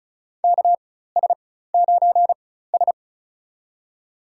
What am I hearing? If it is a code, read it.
Morse code, heard as a keyed tone, about 36 wpm, spaced as for 16 wpm, 700 Hz. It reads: KS9S
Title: USA Call Signs - 4 Characters - 4x Character Spacing - 35wpm